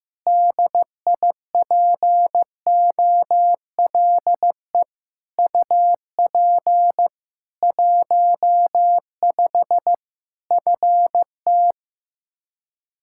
Morse code, heard as a keyed tone, about 15 wpm, 700 Hz. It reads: DIPOLE UP 15 FT
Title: K1USN SST QSO Practice 15wpm